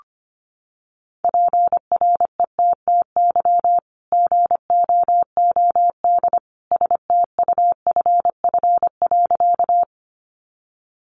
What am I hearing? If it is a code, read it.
Morse code, heard as a keyed tone, about 25 wpm, 700 Hz. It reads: PRETTY GOOD STUFF.
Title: Sentences from Top 1400 Words 25wpm